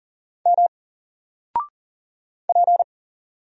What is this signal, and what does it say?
Morse code, heard as a keyed tone, about 40 wpm, 700 Hz. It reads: M P
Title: Single Letter-Number 40wpm